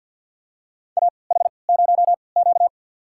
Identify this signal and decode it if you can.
Morse code, heard as a keyed tone, about 53 wpm, spaced as for 29 wpm, 700 Hz. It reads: AH0Q
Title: USA Call Signs - 4 Characters - 3x Character Spacing - 50wpm